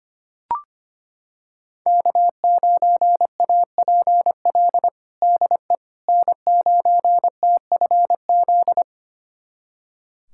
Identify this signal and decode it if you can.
Morse code, heard as a keyed tone, about 25 wpm, 700 Hz. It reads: K9APL DE N9TFZ
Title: USA Callsign DE Callsign - 5 Characters - 25wpm